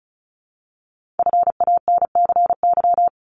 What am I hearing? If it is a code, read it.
Morse code, heard as a keyed tone, about 36 wpm, 700 Hz. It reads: FANCY
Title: Multi-letter Phonemes Repeated 3x 35wpm